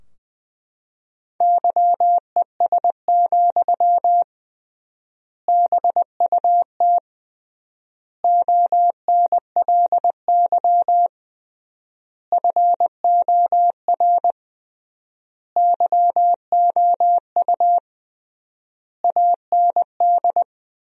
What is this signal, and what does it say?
Morse code, heard as a keyed tone, about 19 wpm, 700 Hz. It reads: YES, BUT ONLY FOR YOU AND
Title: 3x Word-Spacing - Sentences from Top 500 Words - 20wpm